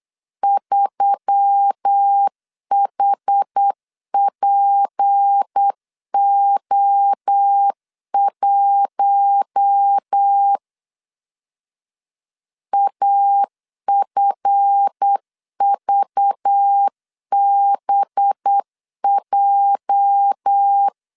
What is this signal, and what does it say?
Morse code, heard as a keyed tone, about 8 wpm, 800 Hz. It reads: 3HPO1 AFVBJ
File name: LEZIONI / 8 YJ45 / 8 YJ 8wpm 3313